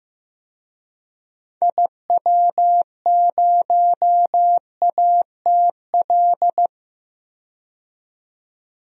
Morse code, heard as a keyed tone, about 15 wpm, 700 Hz. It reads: IW0ATL
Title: DXCC-100 Call Signs - 6 Characters - 15wpm